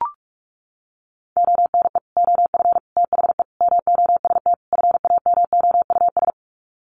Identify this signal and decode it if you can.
Morse code, heard as a keyed tone, about 44 wpm, 700 Hz. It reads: ONE OF THE MOST FAMOUS
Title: Sets of 5 Words Encore2 45wpm